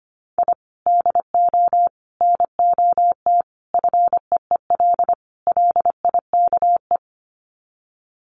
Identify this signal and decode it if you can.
Morse code, heard as a keyed tone, about 25 wpm, 700 Hz. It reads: I DO NOT FEEL LIKE
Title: Sets of 5 Words Encore 25wpm